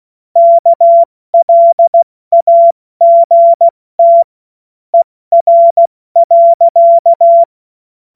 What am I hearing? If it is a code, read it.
Morse code, heard as a keyed tone, about 16 wpm, spaced as for 14 wpm, 680 Hz. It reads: KLAGT ER.
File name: Morsekurs / Witze / Witz 19. Oktober 2021, 70 F80